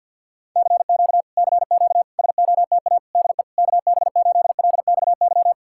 Morse code, heard as a keyed tone, about 67 wpm, spaced as for 58 wpm, 700 Hz. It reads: CQ CQ SOTA DE KB8LXY
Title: Speed-Racing Common QSO Phrases Encore - 50wpm